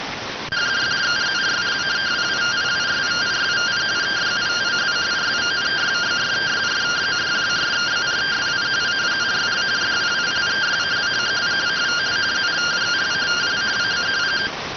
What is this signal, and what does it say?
Two-tone teleprinter signal, about 170 Hz shift.